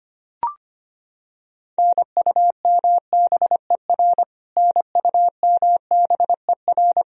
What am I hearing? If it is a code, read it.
Morse code, heard as a keyed tone, about 25 wpm, 700 Hz. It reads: NUMBER NUMBER
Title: Top 100 Words Repeated 25wpm